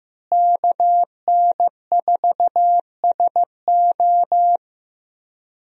Morse code, heard as a keyed tone, about 15 wpm, 700 Hz. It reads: KN4SO
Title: USA Call Signs - 5 Characters - Repeated 3x - 15wpm